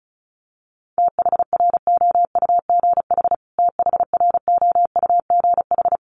Morse code, heard as a keyed tone, about 34 wpm, 700 Hz. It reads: THROUGH THROUGH
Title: Top 200 Words Repeated 35wpm